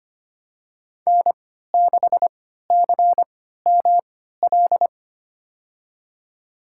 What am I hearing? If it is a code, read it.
Morse code, heard as a keyed tone, about 25 wpm, spaced as for 14 wpm, 700 Hz. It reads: N6CML
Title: USA Call Signs - 5 Characters - 3x Character Spacing - 25wpm